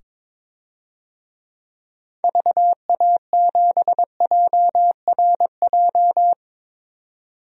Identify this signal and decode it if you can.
Morse code, heard as a keyed tone, about 22 wpm, 700 Hz. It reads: VA7JRJ
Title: DXCC-100 Call Signs - 6 Characters - 22wpm